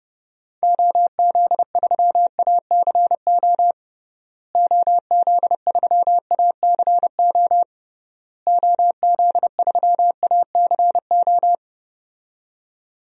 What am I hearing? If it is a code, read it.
Morse code, heard as a keyed tone, about 30 wpm, 700 Hz. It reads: OZ3ACO OZ3ACO OZ3ACO
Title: DXCC-100 Call Signs - 6 Characters - Repeated 3x - 30wpm